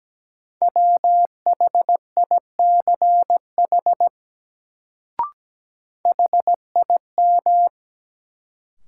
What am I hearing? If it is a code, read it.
Morse code, heard as a keyed tone, about 17 wpm, 700 Hz. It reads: WHICH HIM
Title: Sentences from Top 100 Words - New Word Drill - 17wpm